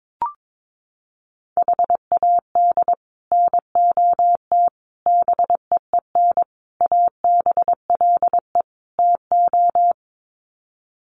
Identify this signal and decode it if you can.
Morse code, heard as a keyed tone, about 22 wpm, 700 Hz. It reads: HAD NOT BEEN ABLE TO
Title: Sets of 5 Words Encore 22wpm